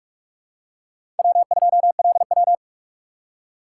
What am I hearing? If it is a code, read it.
Morse code, heard as a keyed tone, about 46 wpm, 700 Hz. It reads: W2LW
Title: USA Call Signs - 4 Characters - 6x Character Spacing - 45wpm